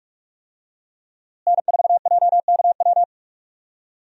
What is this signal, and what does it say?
Morse code, heard as a keyed tone, about 41 wpm, 700 Hz. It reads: N4JKW
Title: USA Call Signs - 5 Characters - Repeated 3x - 45wpm